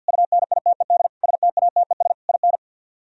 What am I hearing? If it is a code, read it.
Morse code, heard as a keyed tone, about 62 wpm, 690 Hz. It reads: UNITED STATES IN
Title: Speed-Racing - Sets of 5 Words 50wpm